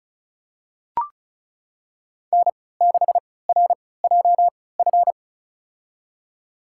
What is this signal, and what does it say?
Morse code, heard as a keyed tone, about 34 wpm, spaced as for 20 wpm, 700 Hz. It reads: N6RJF
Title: USA Call Signs - 5 Characters - 3x Character Spacing - 35wpm